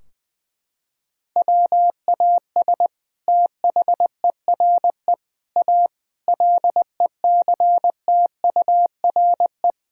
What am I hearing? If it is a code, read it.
Morse code, heard as a keyed tone, about 20 wpm, 700 Hz. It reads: WAS THERE A LECTURE
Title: Sentences from Top 1600 Words 20wpm